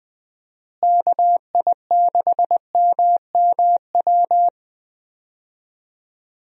Morse code, heard as a keyed tone, about 20 wpm, 700 Hz. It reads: KI6MMW
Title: USA Call Signs - 6 Characters - Repeated 2x - 20wpm